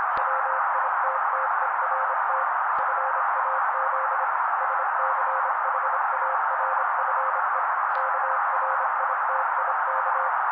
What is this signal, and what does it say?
Morse code, heard as a keyed tone, about 27 wpm, 530 Hz. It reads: WITTERT FAZ SCHARFE KRITIK
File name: daily / 25wpm noise